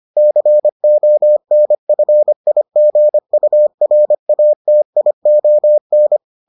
Morse code, heard as a keyed tone, about 25 wpm, 600 Hz. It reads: CONFIGURATION